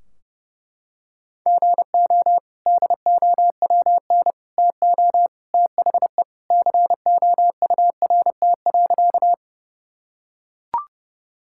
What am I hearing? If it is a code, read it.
Morse code, heard as a keyed tone, about 30 wpm, 700 Hz. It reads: GO DOWN TO THE COURT.
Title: Sentences from Top 1600 Words 30wpm